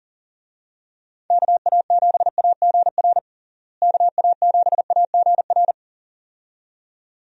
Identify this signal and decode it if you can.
Morse code, heard as a keyed tone, about 40 wpm, 700 Hz. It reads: KA7AGR KA7AGR
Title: USA Call Signs - 6 Characters - Repeated 2x - 40wpm